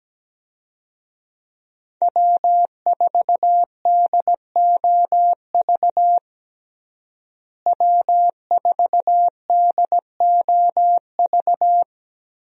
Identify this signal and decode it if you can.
Morse code, heard as a keyed tone, about 17 wpm, 700 Hz. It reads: W4DOV W4DOV
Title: USA Call Signs - 5 Characters - Repeated 3x - 17wpm